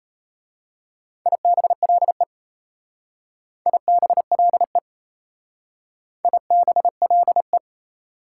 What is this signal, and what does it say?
Morse code, heard as a keyed tone, about 32 wpm, 700 Hz. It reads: IBLE IBLE IBLE
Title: Speed-Racing Prefixes and Suffixes 28wpm